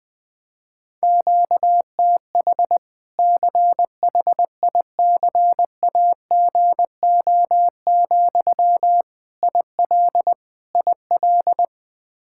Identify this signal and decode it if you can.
Morse code, heard as a keyed tone, about 21 wpm, 700 Hz. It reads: QTH CHICAGO, IL IL
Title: Practice QSO - Exchange 1 - Beginning - 20wpm